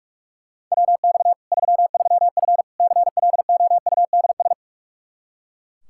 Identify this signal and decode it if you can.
Morse code, heard as a keyed tone, about 42 wpm, 700 Hz. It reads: WX 33F CLOUDS
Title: Practice QSO - Exchange 2 - Info - 45wpm